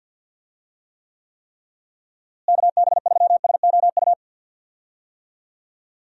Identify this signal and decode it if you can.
Morse code, heard as a keyed tone, about 49 wpm, 700 Hz. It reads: KB3SOU
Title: USA Call Signs - 6 Characters - Repeated 3x - 50wpm